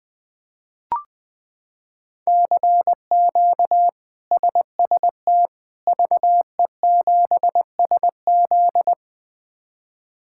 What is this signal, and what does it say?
Morse code, heard as a keyed tone, about 21 wpm, 700 Hz. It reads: CQ SST VE7SZ
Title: K1USN SST QSO Practice - 20wpm